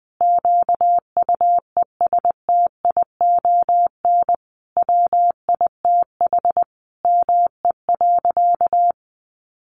Morse code, heard as a keyed tone, about 20 wpm, 700 Hz. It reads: QUESTION WITH ME.
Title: Sentences from Top 1500 Words 20wpm